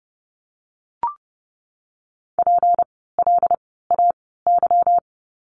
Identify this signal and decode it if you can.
Morse code, heard as a keyed tone, about 30 wpm, spaced as for 17 wpm, 700 Hz. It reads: PLAY
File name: course / 30:10wpm / 237 Lesson 38f - Letter q - Words 30wpm